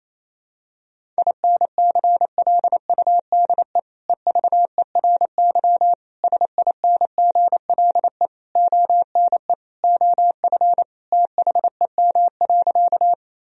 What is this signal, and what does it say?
Morse code, heard as a keyed tone, about 28 wpm, 700 Hz. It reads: INCLUDE EVERY SINGLE ONE OF THEM.
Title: Sentences from Top 1000 Words - Review Part 4 - 28wpm